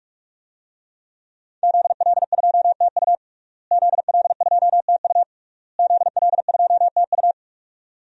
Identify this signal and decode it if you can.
Morse code, heard as a keyed tone, about 41 wpm, 700 Hz. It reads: ZL2TU ZL2TU ZL2TU
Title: DXCC-100 Call Signs - 5 Characters - Repeated 3x - 45wpm